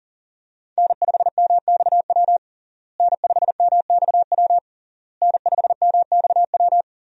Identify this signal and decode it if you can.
Morse code, heard as a keyed tone, about 40 wpm, 700 Hz. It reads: N5MXW N5MXW N5MXW
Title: USA Call Signs - 5 Characters - Repeated 3x - 40wpm